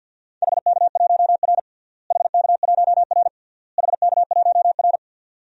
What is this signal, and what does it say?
Morse code, heard as a keyed tone, about 50 wpm, 700 Hz. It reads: HK1R HK1R HK1R
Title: DXCC-100 Call Signs - 4 Characters - Repeated 3x - 50wpm